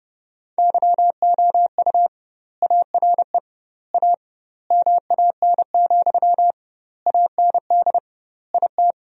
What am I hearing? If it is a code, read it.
Morse code, heard as a keyed tone, about 29 wpm, 700 Hz. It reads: YOU ARE A MAN, AND IT
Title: 2x Word-Spacing - Sentences from Top 200 Words - 30wpm